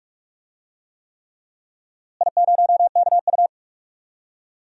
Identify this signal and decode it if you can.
Morse code, heard as a keyed tone, about 41 wpm, 700 Hz. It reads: I0KU
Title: DXCC-100 Call Signs - 4 Characters - Repeated 3x - 45wpm